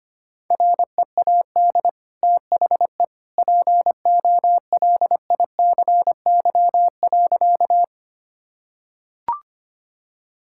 Morse code, heard as a keyed tone, about 25 wpm, 700 Hz. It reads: READ THE POLICY.
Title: Sentences from Top 1000 Words - Review Part 5 - 25wpm